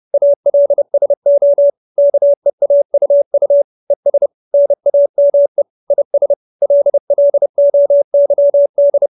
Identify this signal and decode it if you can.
Morse code, heard as a keyed tone, about 30 wpm, 570 Hz. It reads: ALSO KEAUU ES NAME IS LLOYD
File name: QSO311 30